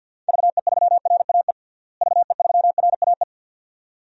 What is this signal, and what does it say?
Morse code, heard as a keyed tone, about 47 wpm, 700 Hz. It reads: VE3RAE VE3RAE
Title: DXCC-100 Call Signs - 6 Characters - Repeated 2x - 50wpm